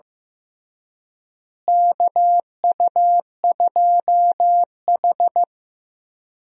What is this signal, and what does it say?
Morse code, heard as a keyed tone, about 15 wpm, 700 Hz. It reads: KU2H
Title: USA Call Signs - 4 Characters - Repeated 3x - 15wpm